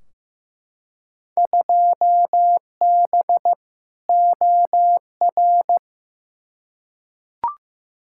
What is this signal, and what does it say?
Morse code, heard as a keyed tone, about 15 wpm, 700 Hz. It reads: 2B OR
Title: Field Day Exchanges Encore - 15wpm